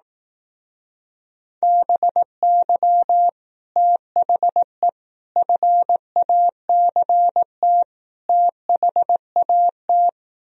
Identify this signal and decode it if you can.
Morse code, heard as a keyed tone, about 18 wpm, 700 Hz. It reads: BY THE FACT THAT
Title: Speed-Racing - Sets of 4 Words 15wpm